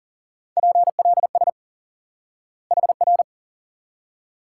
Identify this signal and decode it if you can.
Morse code, heard as a keyed tone, about 38 wpm, 700 Hz. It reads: PLS HR
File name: course / instant-qso-element-course-3 / IQE Course 3 - Lesson 89 - YAGI - Two Elements - 40wpm